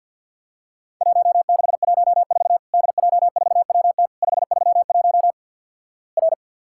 Morse code, heard as a keyed tone, about 50 wpm, 700 Hz. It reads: 1614 DJ4WT 531 R
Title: Worked All Europe DX Contest - QTC Exchange with Repeats - 50wpm